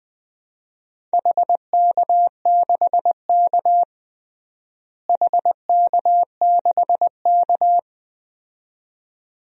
Morse code, heard as a keyed tone, about 21 wpm, 700 Hz. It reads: HK6K HK6K
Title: DXCC-100 Call Signs - 4 Characters - Repeated 2x - 20wpm